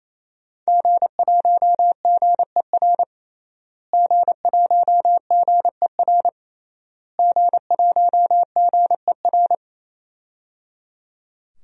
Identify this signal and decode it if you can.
Morse code, heard as a keyed tone, about 28 wpm, 700 Hz. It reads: G1GER G1GER G1GER
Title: DXCC-100 Call Signs - 5 Characters - Repeated 3x - 28wpm